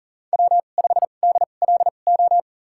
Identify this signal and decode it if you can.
Morse code, heard as a keyed tone, about 40 wpm, spaced as for 29 wpm, 700 Hz. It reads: W5DLO
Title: USA Call Signs - 5 Characters - 2x Character Spacing - 40wpm